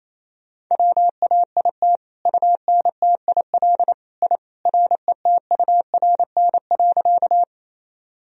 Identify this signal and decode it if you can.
Morse code, heard as a keyed tone, about 28 wpm, 700 Hz. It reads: WAIT UNTIL I RETURN.
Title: Sentences from Top 1000 Words - Review Part 1 - 28wpm